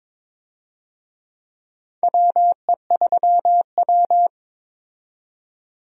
Morse code, heard as a keyed tone, about 22 wpm, 700 Hz. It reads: WE3W
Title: USA Call Signs - 4 Characters - 22wpm